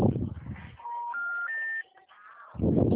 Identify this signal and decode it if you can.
Special information tone, with noise.
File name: out- -2001- - - .572